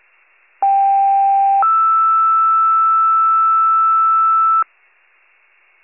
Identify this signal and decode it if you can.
Two-tone page, heard as a one-second tone, then a three-second tone, about 790 then 1320 Hz. It is heard from 0.6 to 4.6 s.
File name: wcalert